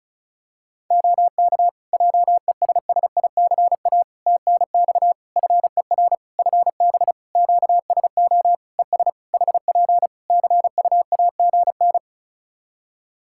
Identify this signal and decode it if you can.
Morse code, heard as a keyed tone, about 36 wpm, 700 Hz. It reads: OK JESSICA TNX FER FB QSO ES HP CUAGN
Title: Practice QSO - Exchange 4 - Ending - 35wpm